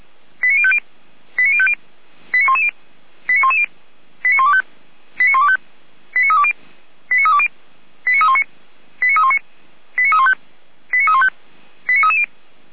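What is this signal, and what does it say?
Five-tone selective call (ZVEI), heard as quick runs of five tones, tones of about 70 ms. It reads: 89050 89050 89110 89110 89125 89125 89320 89320 89319 89319 89315 89315 89330
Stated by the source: ZVEI five-tone selective call received on an SDR, recorded on the Feuerwehr Orlamünde fire brigade server.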